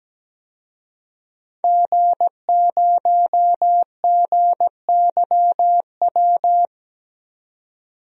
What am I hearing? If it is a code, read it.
Morse code, heard as a keyed tone, about 17 wpm, 700 Hz. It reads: G0GYW
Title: DXCC-100 Call Signs - 5 Characters - 17wpm